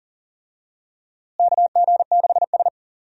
Morse code, heard as a keyed tone, about 40 wpm, 700 Hz. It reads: KG6S
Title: USA Call Signs - 4 Characters - Repeated 3x - 40wpm